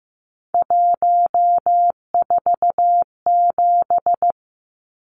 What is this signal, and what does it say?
Morse code, heard as a keyed tone, about 15 wpm, 700 Hz. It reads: 147
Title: Numbers - 3-Digit 15wpm